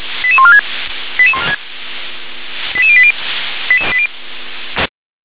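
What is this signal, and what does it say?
Five-tone selective call (ZVEI), heard as quick runs of five tones, tones of about 70 ms. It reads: 88146 88080
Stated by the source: ZVEI five-tone selective call 88702 received by SDR.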